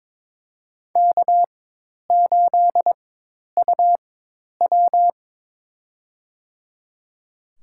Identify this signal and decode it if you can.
Morse code, heard as a keyed tone, about 22 wpm, spaced as for 10 wpm, 700 Hz. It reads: K8UW
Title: USA Call Signs - 4 Characters - 4x Character Spacing - 22wpm